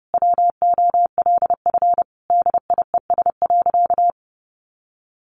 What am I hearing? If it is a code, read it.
Morse code, heard as a keyed tone, about 30 wpm, 700 Hz. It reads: WOLF DIES.
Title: Additional Easy Phrases 30wpm